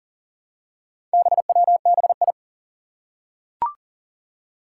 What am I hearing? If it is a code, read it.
Morse code, heard as a keyed tone, about 37 wpm, 700 Hz. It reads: BWBI
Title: Four Challenging Letter-Number Combinations 40wpm